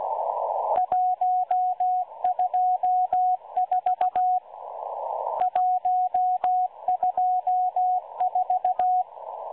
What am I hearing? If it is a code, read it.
Morse code, heard as a keyed tone, about 16 wpm, 730 Hz, with radio noise. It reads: 124 124
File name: M12-Morse-Numbers-Station-8116-kHz-CW